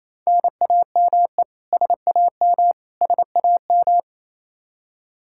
Morse code, heard as a keyed tone, about 28 wpm, 700 Hz. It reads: NAME SAM SAM
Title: Practice QSO - Exchange 1 - Beginning - 28wpm